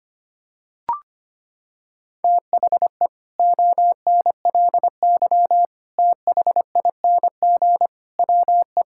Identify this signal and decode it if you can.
Morse code, heard as a keyed tone, about 25 wpm, 700 Hz. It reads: THE ONLY THING WE
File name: Sets of 9 Words - 25wpm